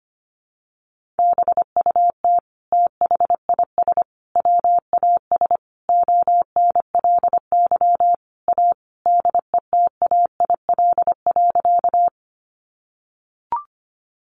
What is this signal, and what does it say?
Morse code, heard as a keyed tone, about 25 wpm, 700 Hz. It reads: BUT THIS WAS ONLY A DETAIL.
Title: Sentences from Top 1500 Words 25wpm